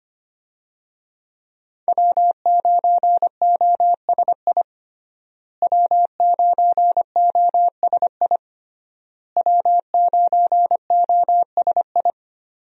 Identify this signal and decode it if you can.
Morse code, heard as a keyed tone, about 25 wpm, 700 Hz. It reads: W9OSI W9OSI W9OSI
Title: USA Call Signs - 5 Characters - Repeated 3x - 25wpm